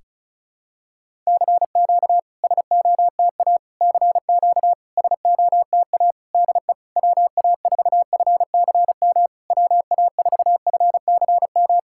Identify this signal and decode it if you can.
Morse code, heard as a keyed tone, about 34 wpm, 700 Hz. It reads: CQ SOTA CQ SOTA DE WA4FCM WA4FCM
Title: SOTA QSO Practice 35wpm